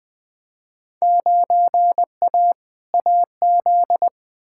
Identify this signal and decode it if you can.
Morse code, heard as a keyed tone, about 20 wpm, 700 Hz. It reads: 9A AZ
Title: Field Day Exchanges - 20wpm